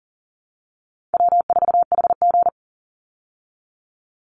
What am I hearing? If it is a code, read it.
Morse code, heard as a keyed tone, about 40 wpm, 700 Hz. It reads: W4HG
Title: Easy Call Signs Repeated 2x - 40wpm